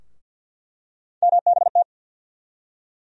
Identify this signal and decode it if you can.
Morse code, heard as a keyed tone, about 48 wpm, 700 Hz. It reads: MBT
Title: USA Call Sign Postfixes - 3 Letters 50wpm